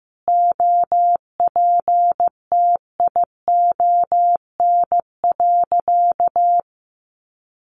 Morse code, heard as a keyed tone, about 15 wpm, 700 Hz. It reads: OPTION.
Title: Sentences from Top 1300 Words 15wpm